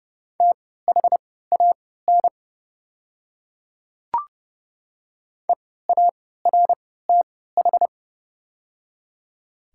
Morse code, heard as a keyed tone, about 30 wpm, spaced as for 17 wpm, 690 Hz. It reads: THAN EARTH
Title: 3x Character Spacing - Top 200 Words - 30wpm